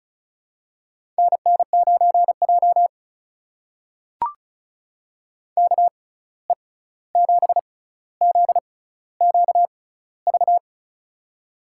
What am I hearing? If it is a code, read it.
Morse code, heard as a keyed tone, about 34 wpm, spaced as for 16 wpm, 700 Hz. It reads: NN9J KE7ZQV
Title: USA Call Signs - 6x Character Spacing - 35wpm